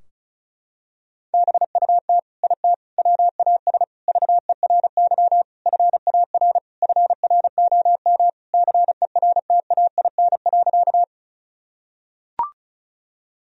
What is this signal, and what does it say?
Morse code, heard as a keyed tone, about 36 wpm, 700 Hz. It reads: BUT IT WAS VERY FAR FROM CERTAIN.
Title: Sentences from Top 1000 Words - Review Part 2 - 35wpm